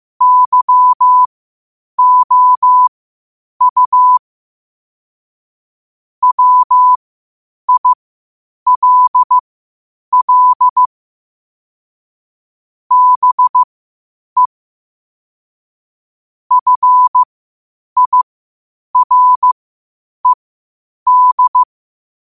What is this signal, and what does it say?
Morse code, heard as a keyed tone, about 15 wpm, spaced as for 8 wpm, 1000 Hz. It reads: YOU WILL BE FIRED
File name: qotdfiles / 8wpm